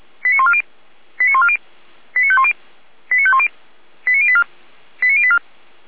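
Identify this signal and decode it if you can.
Five-tone selective call (ZVEI), heard as quick runs of five tones, tones of about 70 ms. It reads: 89140 89140 89510 89510 89084 89084
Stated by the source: ZVEI five-tone selective call 89120, received by SDR.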